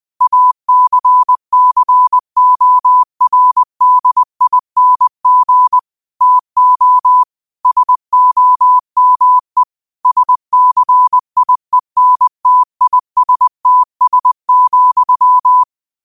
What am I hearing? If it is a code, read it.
Morse code, heard as a keyed tone, about 20 wpm, 1000 Hz. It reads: ACCORDING TO SOME SCIENTISTS,
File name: qotdfiles / 20wpm